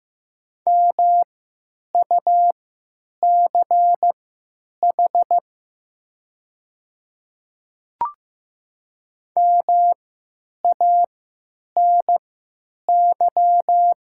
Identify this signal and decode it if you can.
Morse code, heard as a keyed tone, about 15 wpm, spaced as for 9 wpm, 700 Hz. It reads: MUCH MANY
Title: 3x Character Spacing - Top 500 Words - 15wpm